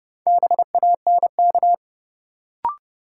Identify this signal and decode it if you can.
Morse code, heard as a keyed tone, about 30 wpm, 700 Hz. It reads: BANK
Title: Sets of 5 Words - New Word Drill - 30wpm